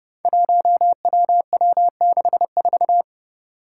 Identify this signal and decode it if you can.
Morse code, heard as a keyed tone, about 30 wpm, 700 Hz. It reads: 1WW64
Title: Five Challenging Letter-Number Combinations 30wpm